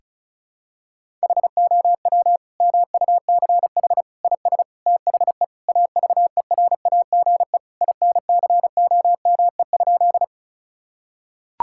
Morse code, heard as a keyed tone, about 36 wpm, 700 Hz. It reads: HOW MUCH IS THE AVERAGE INCOME?
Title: Sentences from Top 1000 Words - Review Part 6 - 35wpm